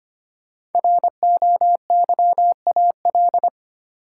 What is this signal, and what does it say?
Morse code, heard as a keyed tone, about 25 wpm, 700 Hz. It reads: ROYAL
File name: 5 Letter Words - First 3 Letters Given - 25wpm